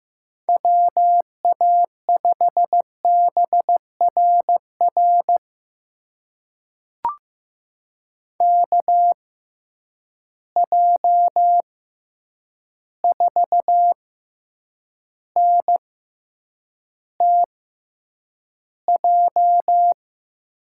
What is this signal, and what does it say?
Morse code, heard as a keyed tone, about 15 wpm, spaced as for 8 wpm, 700 Hz. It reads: WA5BRR KJ4NTJ
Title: USA Call Signs - 6x Character Spacing - 15wpm